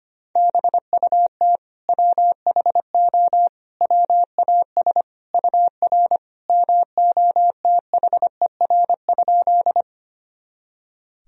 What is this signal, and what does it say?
Morse code, heard as a keyed tone, about 25 wpm, 700 Hz. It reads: BUT WHO WAS UR MOTHER?
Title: Sentences from Top 200 Words and Abbreviations 25wpm